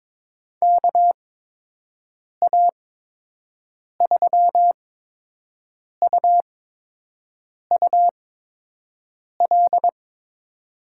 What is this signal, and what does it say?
Morse code, heard as a keyed tone, about 22 wpm, spaced as for 6 wpm, 700 Hz. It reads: KA3UUL
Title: USA Call Signs - 8x Character Spacing - 22wpm